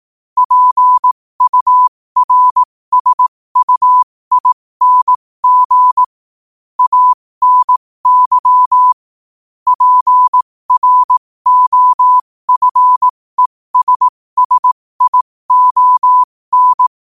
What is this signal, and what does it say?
Morse code, heard as a keyed tone, about 18 wpm, spaced as for 15 wpm, 1000 Hz. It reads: PURSUING ANY PROFESSION
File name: qotdfiles / 15wpm